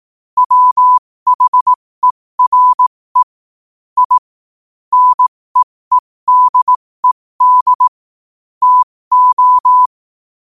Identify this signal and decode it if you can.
Morse code, heard as a keyed tone, about 18 wpm, spaced as for 15 wpm, 1000 Hz. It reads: WHERE I NEEDED TO
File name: qotdfiles / 15wpm